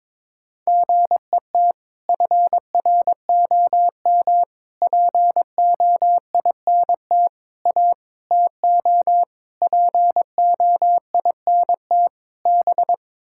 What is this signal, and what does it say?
Morse code, heard as a keyed tone, about 22 wpm, 700 Hz. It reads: GET FROM POINT A TO POINT B
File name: practice / 22wpm / Sets of 7 Words - 22wpm